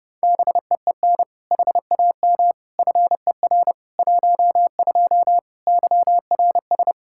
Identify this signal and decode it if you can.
Morse code, heard as a keyed tone, about 32 wpm, 700 Hz. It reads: BEEN HAM FER 12 YRS
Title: Practice QSO - Exchange 3 - More Info - 30wpm